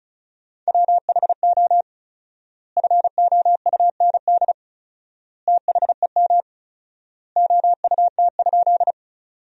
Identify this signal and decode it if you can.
Morse code, heard as a keyed tone, about 34 wpm, 700 Hz. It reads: WHO FOUND THEM OUT?
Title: 4x Word-Spacing - Sentences from Top 300 Words - 35wpm